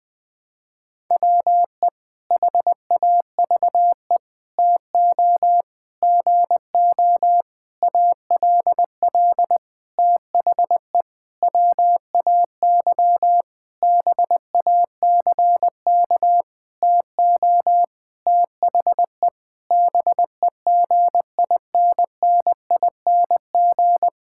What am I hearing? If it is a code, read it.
Morse code, heard as a keyed tone, about 20 wpm, 700 Hz. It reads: WE HAVE TO GO ALL THE WAY BACK TO THE BEGINNING
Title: Sentences with 8-Word Phrases - Spoken Beginning - 20wpm